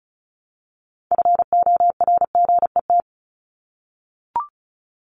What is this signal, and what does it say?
Morse code, heard as a keyed tone, about 34 wpm, 700 Hz. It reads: FORGET
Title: Top 1001-1250 Words 35wpm